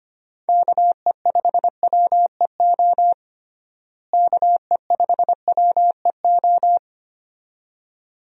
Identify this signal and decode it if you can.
Morse code, heard as a keyed tone, about 25 wpm, 700 Hz. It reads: KE5WEO KE5WEO
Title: USA Call Signs - 6 Characters - Repeated 2x - 25wpm